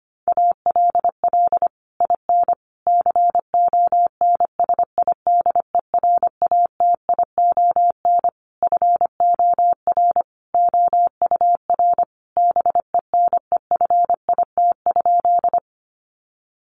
Morse code, heard as a keyed tone, about 25 wpm, 700 Hz. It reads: ALL IN CONSIDERATION FOR OUR BENEFIT?
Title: Sentences from Top 1500 Words 25wpm